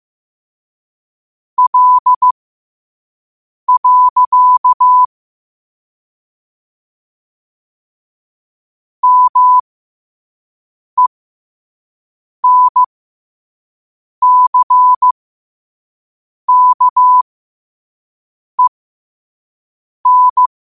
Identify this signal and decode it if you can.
Morse code, heard as a keyed tone, about 15 wpm, spaced as for 5 wpm, 1000 Hz. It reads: L. MENCKEN
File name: qotdfiles / 5wpm